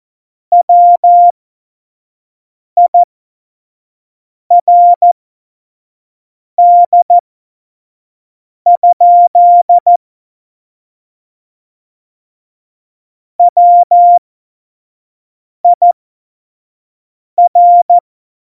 Morse code, heard as a keyed tone, about 14 wpm, spaced as for 5 wpm, 700 Hz. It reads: WIRD? WIR
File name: Morsekurs / Witze / Witz 19. Januar 2021, 25 F70